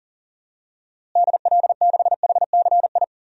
Morse code, heard as a keyed tone, about 40 wpm, 700 Hz. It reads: DL6HCI
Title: DXCC-100 Call Signs - 6 Characters - Repeated 3x - 40wpm